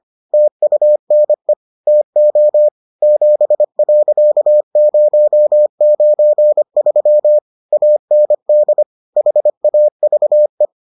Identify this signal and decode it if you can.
Morse code, heard as a keyed tone, about 25 wpm, 600 Hz. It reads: TUNE TO 7.093 AND HAVE